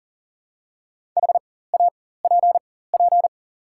Morse code, heard as a keyed tone, about 40 wpm, spaced as for 19 wpm, 690 Hz. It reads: HAPP
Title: 4x Character Spacing - Top 500 Words - 40wpm